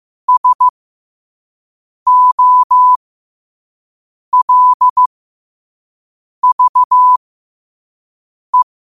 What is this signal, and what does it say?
Morse code, heard as a keyed tone, about 15 wpm, spaced as for 5 wpm, 1000 Hz. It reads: SOLVE
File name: 5wpm